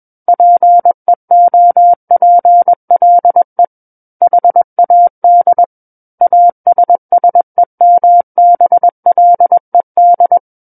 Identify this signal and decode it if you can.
Morse code, heard as a keyed tone, about 21 wpm, 700 Hz. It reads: PEOPLE HAD ASSEMBLED
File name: Book One / 12-Chapter 12 - 21 WPM